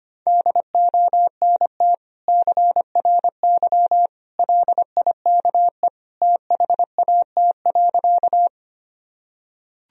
Morse code, heard as a keyed tone, about 25 wpm, 700 Hz. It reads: DONT CRY LIKE THAT.
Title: Sentences from Top 1000 Words - Review Part 2 - 25wpm